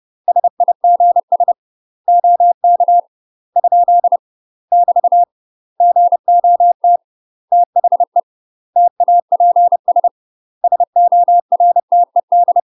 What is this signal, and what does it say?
Morse code, heard as a keyed tone, about 30 wpm, 700 Hz. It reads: SIGS OK ? <BT> GOT THE TAPS SORTED
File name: UK MASTER QSOs TRAINING / 30WPM / UK MASTER QSOs 0018